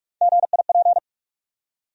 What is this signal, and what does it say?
Morse code, heard as a keyed tone, about 44 wpm, 700 Hz. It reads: GIP
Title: USA Call Sign Postfixes - 3 Letters 45wpm